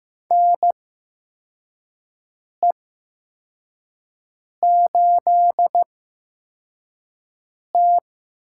Morse code, heard as a keyed tone, about 15 wpm, spaced as for 4 wpm, 700 Hz. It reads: NE8T
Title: USA Call Signs - 4 Characters - 8x Character Spacing - 15wpm